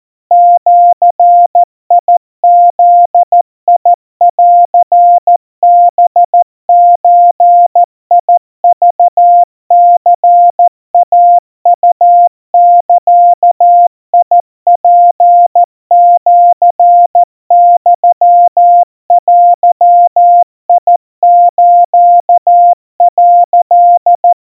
Morse code (Japanese Wabun code), heard as a keyed tone, about 14 wpm, 700 Hz. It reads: ジブンハゾクニイウザツシユデス」
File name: kamyu 50 cpm 700 Hz Crispy